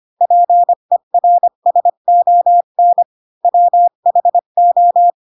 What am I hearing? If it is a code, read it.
Morse code, heard as a keyed tone, about 25 wpm, 700 Hz. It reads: PERSON WHO
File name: Pinocchio / Pinocchio ch5 20 25 ik1hge